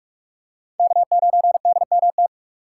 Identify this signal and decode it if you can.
Morse code, heard as a keyed tone, about 46 wpm, 700 Hz. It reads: K9DMT
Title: USA Call Signs - 5 Characters - 3x Character Spacing - 45wpm